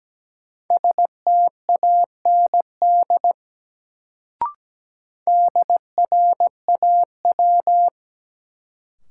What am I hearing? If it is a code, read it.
Morse code, heard as a keyed tone, about 17 wpm, 700 Hz. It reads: STAND DRAW
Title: Sentences from Top 300 Words - New Word Drill - 17wpm